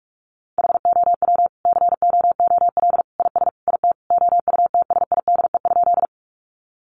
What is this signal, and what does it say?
Morse code, heard as a keyed tone, about 44 wpm, 700 Hz. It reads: HOW COOL IS IT OUTSIDE?
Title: Sentences from Top 1200 Words 45wpm